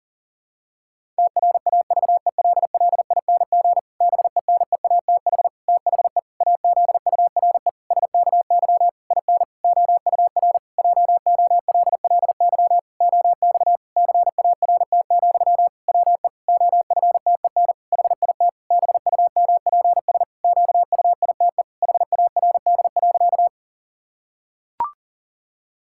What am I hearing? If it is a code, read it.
Morse code, heard as a keyed tone, about 37 wpm, 700 Hz. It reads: TRAVELLING BENEATH THE AZURE SKY IN OUR JOLLY OX CART, WE OFTEN HIT BUMPS QUITE HARD.
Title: Pangram Challenge v2 - 40wpm